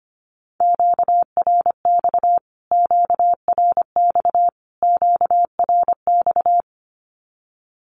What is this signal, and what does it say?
Morse code, heard as a keyed tone, about 25 wpm, 700 Hz. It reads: QRX QRX QRX
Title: Q-Codes Repeated 3x 25wpm